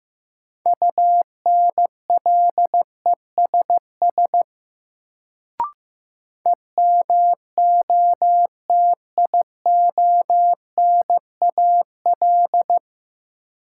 Morse code, heard as a keyed tone, about 16 wpm, 700 Hz. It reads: UNLESS EMOTIONAL
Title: Sentences from Top 1000 Words - Review Part 5 - New Word Drill - 15wpm